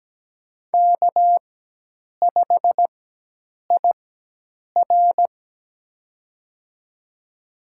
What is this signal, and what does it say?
Morse code, heard as a keyed tone, about 17 wpm, spaced as for 8 wpm, 700 Hz. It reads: K5IR
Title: USA Call Signs - 4 Characters - 4x Character Spacing - 17wpm